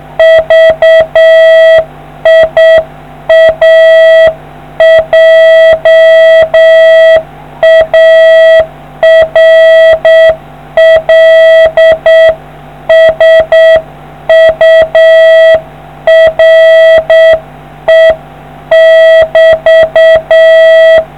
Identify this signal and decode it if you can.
Morse code, heard as a keyed tone, about 7 wpm, 650 Hz. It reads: VIA JARL SURE <BT>